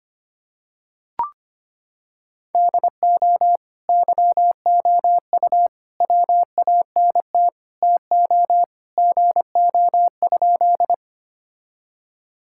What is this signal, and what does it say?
Morse code, heard as a keyed tone, about 25 wpm, 700 Hz. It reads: DO YOU WANT TO GO?
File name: practice / 25wpm / Sentences with 3-Word Phrases - 25wpm